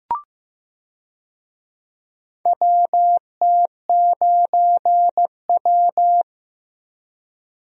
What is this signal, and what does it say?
Morse code, heard as a keyed tone, about 15 wpm, 700 Hz. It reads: WT9W
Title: USA Call Signs - 4 Characters - Repeated 3x - 15wpm